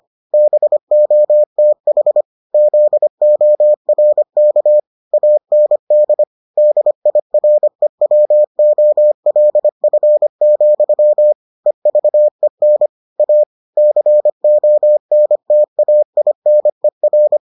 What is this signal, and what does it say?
Morse code, heard as a keyed tone, about 25 wpm, 600 Hz. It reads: BOTH ZORK AND DIREWOLF, EVEN A CONTAINER